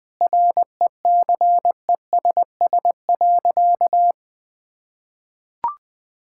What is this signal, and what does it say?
Morse code, heard as a keyed tone, about 20 wpm, 700 Hz. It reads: RECESS.
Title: Easy Sentences with Hints - 20wpm